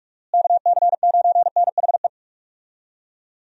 Morse code, heard as a keyed tone, about 44 wpm, 700 Hz. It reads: KC9NHE
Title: USA Call Signs - 6 Characters - Repeated 3x - 45wpm